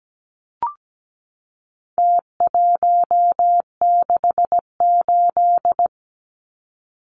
Morse code, heard as a keyed tone, about 17 wpm, 700 Hz. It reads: T168
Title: Numbers - Contest Style - 4-Digit 17wpm